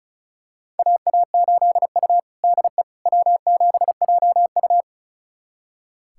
Morse code, heard as a keyed tone, about 34 wpm, 700 Hz. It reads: AA8U DE W7JU
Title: USA Callsign DE Callsign 35wpm